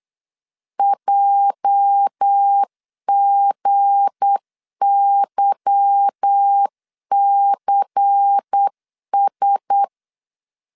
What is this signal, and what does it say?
Morse code, heard as a keyed tone, about 8 wpm, 800 Hz. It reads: JGYCS
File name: CW / DET / 1 9 1500C S A 8wpm 5637